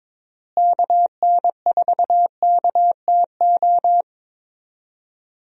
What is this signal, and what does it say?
Morse code, heard as a keyed tone, about 22 wpm, 700 Hz. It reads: KN4KTO
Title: USA Call Signs - 6 Characters - Repeated 3x - 22wpm